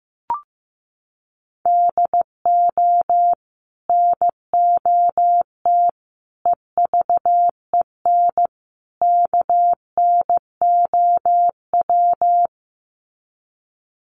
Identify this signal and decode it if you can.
Morse code, heard as a keyed tone, about 15 wpm, 700 Hz. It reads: DO NOT EVEN KNOW
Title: Sets of 4 Words 15wpm